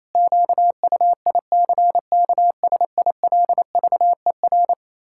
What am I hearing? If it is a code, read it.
Morse code, heard as a keyed tone, about 28 wpm, 700 Hz. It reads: QUICKSILVER
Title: Pangram Challenge v2 - 28wpm